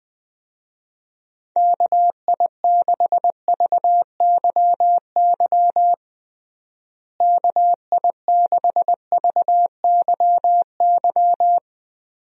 Morse code, heard as a keyed tone, about 20 wpm, 700 Hz. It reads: KI6VYY KI6VYY
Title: USA Call Signs - 6 Characters - Repeated 3x - 20wpm